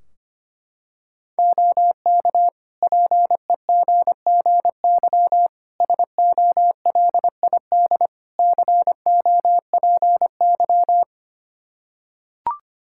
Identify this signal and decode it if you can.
Morse code, heard as a keyed tone, about 25 wpm, 700 Hz. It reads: OK PEGGY SOLID COPY
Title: Practice QSO - Exchange 3 - More Info - 25wpm